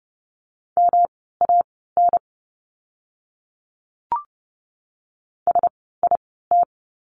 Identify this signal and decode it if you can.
Morse code, heard as a keyed tone, about 30 wpm, spaced as for 17 wpm, 700 Hz. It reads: MAN SIT
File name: course / 30:10wpm / 108 Lesson 18f - Letter m - Words 30wpm